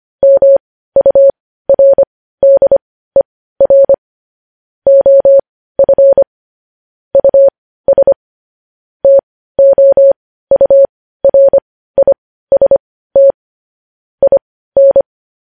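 Morse code, heard as a keyed tone, about 25 wpm, spaced as for 15 wpm, 550 Hz. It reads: MURDER OF US TOURIST IN
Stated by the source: News headlines sent in Morse code at 15 wpm.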